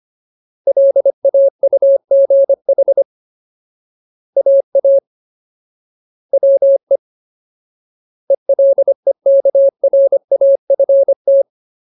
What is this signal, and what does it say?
Morse code, heard as a keyed tone, about 25 wpm, 550 Hz. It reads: LAUGH AA WE ELEKRAFT
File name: Ultimate Word List / 25WPM Ultimate